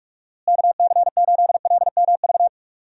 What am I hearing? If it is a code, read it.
Morse code, heard as a keyed tone, about 46 wpm, 700 Hz. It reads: KC8LMV
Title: USA Call Signs - 6 Characters - Repeated 2x - 45wpm